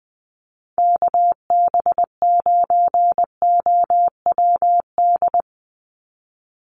Morse code, heard as a keyed tone, about 20 wpm, 700 Hz. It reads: KB9OWD
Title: Call Signs - 20wpm